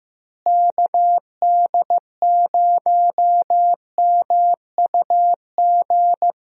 Morse code, heard as a keyed tone, about 15 wpm, 700 Hz. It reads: KD0MUG
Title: USA Call Signs - 6 Characters - Repeated 3x - 15wpm